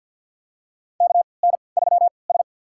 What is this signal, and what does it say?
Morse code, heard as a keyed tone, about 46 wpm, spaced as for 27 wpm, 700 Hz. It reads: KN3S
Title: USA Call Signs - 3x Character Spacing - 50wpm